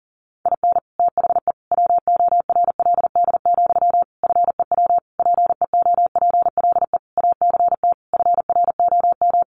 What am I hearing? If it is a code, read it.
Morse code, heard as a keyed tone, about 40 wpm, 700 Hz. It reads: IN THE WORLD, FEW PEOPLE ACT FROM
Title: Sentences from Top 1400 Words 40wpm